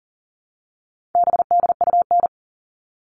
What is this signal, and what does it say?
Morse code, heard as a keyed tone, about 40 wpm, 700 Hz. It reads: BDUN
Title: All Cut Numbers - 4-Digit 40wpm